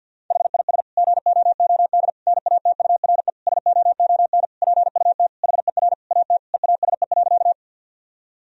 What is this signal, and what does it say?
Morse code, heard as a keyed tone, about 53 wpm, 700 Hz. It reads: HIS GOOD NATURE SOON PUT HER AT EASE.
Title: Sentences from Top 1000 Words - Review Part 1 - 50wpm